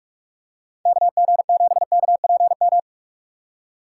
Morse code, heard as a keyed tone, about 42 wpm, 690 Hz. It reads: KG7KPM
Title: USA Call Signs - 6 Characters - 3x Character Spacing - 45wpm